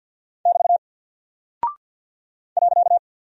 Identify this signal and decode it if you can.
Morse code, heard as a keyed tone, about 49 wpm, 700 Hz. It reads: <BT> .
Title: Single Character (Letter:Number:Punctuation:Prosign) Alt 50wpm